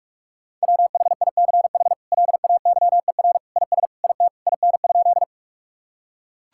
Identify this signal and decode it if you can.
Morse code, heard as a keyed tone, about 41 wpm, 700 Hz. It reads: WHICH LAYER IS IT IN?
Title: Sentences from Top 1600 Words 45wpm